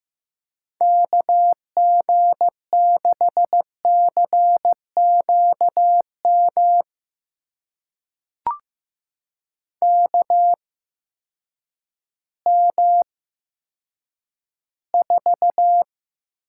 Morse code, heard as a keyed tone, about 15 wpm, spaced as for 9 wpm, 700 Hz. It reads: KG6CQM KM4
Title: USA Call Signs - 6 Characters - 8x Character Spacing - 15wpm